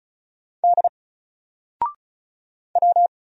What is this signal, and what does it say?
Morse code, heard as a keyed tone, about 36 wpm, 700 Hz. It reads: D W
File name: Single-Letter-Number-35wpm